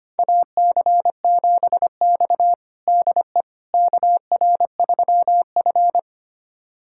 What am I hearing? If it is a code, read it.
Morse code, heard as a keyed tone, about 25 wpm, 700 Hz. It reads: AC7X DE KR3F
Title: USA Callsign DE Callsign - 4 Characters - 25wpm